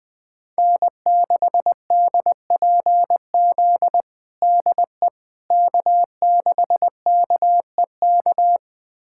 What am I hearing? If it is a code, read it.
Morse code, heard as a keyed tone, about 21 wpm, 700 Hz. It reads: N6DPZ DE K6KEK
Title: USA Callsign DE Callsign - 5 Characters - 20wpm